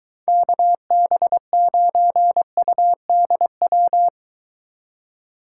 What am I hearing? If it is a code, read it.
Morse code, heard as a keyed tone, about 22 wpm, 700 Hz. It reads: KB9UDW